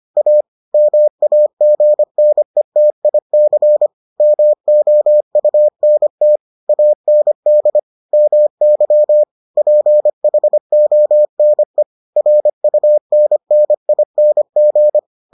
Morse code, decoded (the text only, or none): A MAGNETIC MOUNT AND MY PHONE RUNNING